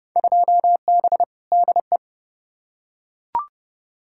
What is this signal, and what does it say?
Morse code, heard as a keyed tone, about 30 wpm, 700 Hz. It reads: 2B DE
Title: Field Day Exchanges Encore - 30wpm